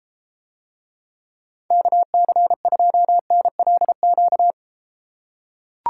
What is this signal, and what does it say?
Morse code, heard as a keyed tone, about 33 wpm, 700 Hz. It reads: KC2NLQ